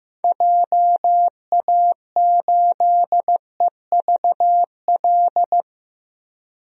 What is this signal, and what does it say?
Morse code, heard as a keyed tone, about 15 wpm, 700 Hz. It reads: JA8EVL
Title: DXCC-100 Call Signs - 6 Characters - 15wpm